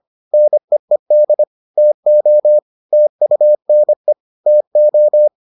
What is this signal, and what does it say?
Morse code, heard as a keyed tone, about 25 wpm, 600 Hz. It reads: NEED TO TUNE TO